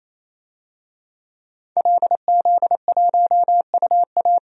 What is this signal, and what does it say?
Morse code, heard as a keyed tone, about 28 wpm, 700 Hz. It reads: LZ1UA
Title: DXCC-100 Call Signs - 5 Characters - 28wpm